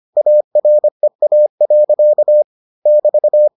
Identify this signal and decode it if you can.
Morse code, heard as a keyed tone, about 25 wpm, 600 Hz. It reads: AREA. <BT>